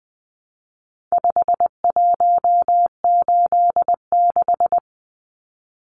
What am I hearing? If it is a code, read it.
Morse code, heard as a keyed tone, about 20 wpm, 700 Hz. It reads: 5186
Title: Numbers - 4-Digit 20wpm